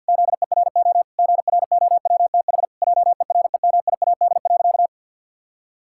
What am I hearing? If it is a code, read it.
Morse code, heard as a keyed tone, about 53 wpm, 700 Hz. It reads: ZERO GROWTH JEREMIAD.
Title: Pangram Challenge v2 - 50wpm